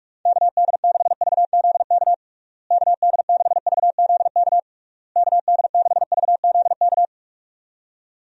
Morse code, heard as a keyed tone, about 46 wpm, 700 Hz. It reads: KD6VZK KD6VZK KD6VZK
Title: USA Call Signs - 6 Characters - Repeated 3x - 45wpm